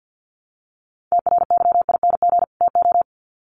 Morse code, heard as a keyed tone, about 50 wpm, 700 Hz. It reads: TRYING TO
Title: Sets of 2 Words 50wpm